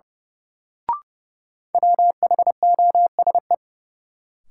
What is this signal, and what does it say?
Morse code, heard as a keyed tone, about 30 wpm, 700 Hz. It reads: WHOSE
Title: Sentences from Top 1200 Words - New Word Drill - 30wpm